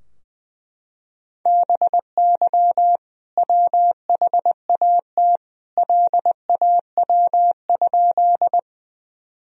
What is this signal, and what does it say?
Morse code, heard as a keyed tone, about 21 wpm, 700 Hz. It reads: BY WHAT LAW?
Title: Sentences from Top 1000 Words - Review Part 3 - 20wpm